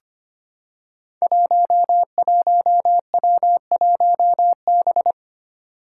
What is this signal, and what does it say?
Morse code, heard as a keyed tone, about 25 wpm, 700 Hz. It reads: 11W1B
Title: Five Challenging Letter-Number Combinations 25wpm